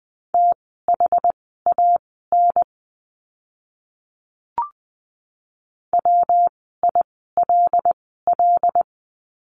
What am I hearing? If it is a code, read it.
Morse code, heard as a keyed tone, about 20 wpm, spaced as for 14 wpm, 700 Hz. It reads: THAN WILL
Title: 230 Lesson 37f - Letter x - Words 20wpm